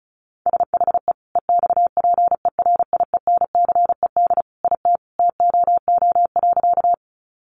Morse code, heard as a keyed tone, about 34 wpm, 700 Hz. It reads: SHE EXPERIENCED IT TOO.
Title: Sentences from Top 1100 Words 35wpm